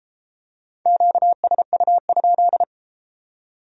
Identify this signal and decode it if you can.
Morse code, heard as a keyed tone, about 33 wpm, 700 Hz. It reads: QSU?